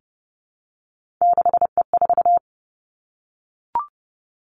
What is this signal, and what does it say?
Morse code, heard as a keyed tone, about 30 wpm, 700 Hz. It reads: 6E4
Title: Three Challenging Letter-Number Combinations 30wpm